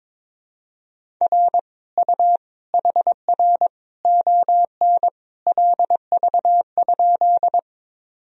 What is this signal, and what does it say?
Morse code, heard as a keyed tone, about 22 wpm, 700 Hz. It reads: R U HR ON LV?
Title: Sentences from Top 400 Words and Abbreviations 22wpm